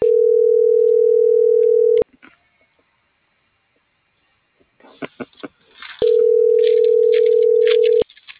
Ringback tone (North America), 440 + 480 Hz.
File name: CENTRAL MABE / out xquenda.trinidad